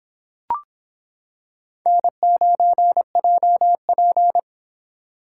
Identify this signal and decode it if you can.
Morse code, heard as a keyed tone, about 25 wpm, 700 Hz. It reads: N9JP